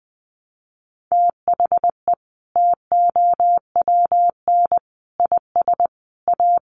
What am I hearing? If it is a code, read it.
Morse code, heard as a keyed tone, about 20 wpm, 700 Hz. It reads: THE TOWN IS A
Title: Sentences from Top 1500 Words 20wpm